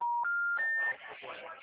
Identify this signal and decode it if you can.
Special information tone, with speech.
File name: out- -2012- - - .31268